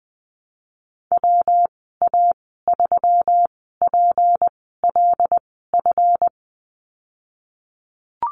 Morse code, heard as a keyed tone, about 20 wpm, spaced as for 15 wpm, 700 Hz. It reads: WA3PLF
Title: 179 Lesson 29g - Slash - Call Signs 20wpm